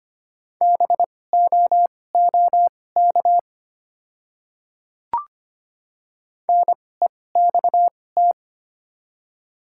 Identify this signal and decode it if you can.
Morse code, heard as a keyed tone, about 25 wpm, spaced as for 18 wpm, 700 Hz. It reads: BOOK NEXT
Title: 2x Character Spacing - Top 300 Words - 25wpm